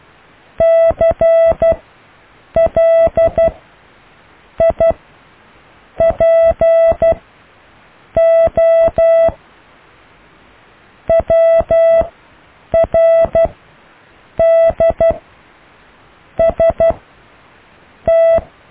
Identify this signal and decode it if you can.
Morse code, heard as a keyed tone, about 12 wpm, spaced as for 6 wpm, 660 Hz. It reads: CLIPO WRDST
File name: cw / IJLUM+CDRO+EKTPV+BFHNW